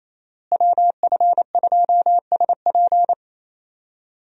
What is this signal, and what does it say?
Morse code, heard as a keyed tone, about 27 wpm, 700 Hz. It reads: WF2SP
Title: USA Call Signs - 5 Characters - Repeated 3x - 28wpm